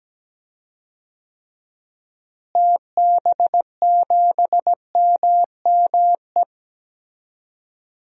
Morse code, heard as a keyed tone, about 17 wpm, 700 Hz. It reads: TB7MME
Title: DXCC-100 Call Signs - 6 Characters - Repeated 3x - 17wpm